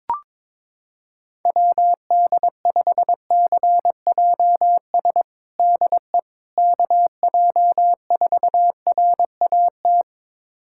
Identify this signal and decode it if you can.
Morse code, heard as a keyed tone, about 22 wpm, 700 Hz. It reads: WD5CJS DE KJ4RAT
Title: USA Callsign DE Callsign 22wpm